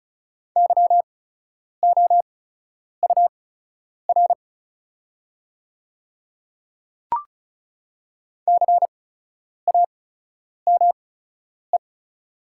Morse code, heard as a keyed tone, about 36 wpm, spaced as for 10 wpm, 690 Hz. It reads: YOUR CAME
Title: 8x Character Spacing - Top 300 Words - 35wpm